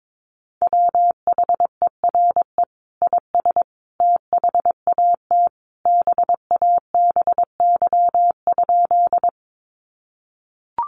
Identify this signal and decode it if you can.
Morse code, heard as a keyed tone, about 22 wpm, 700 Hz. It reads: WHERE IS THAT BABY?
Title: Sentences from Top 1300 Words 22wpm